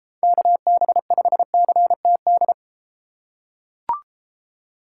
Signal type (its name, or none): Morse code